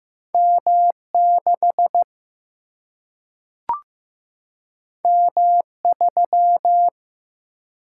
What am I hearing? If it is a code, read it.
Morse code, heard as a keyed tone, about 15 wpm, 700 Hz. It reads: M6 M3
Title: Single-Letter Single-Number 15wpm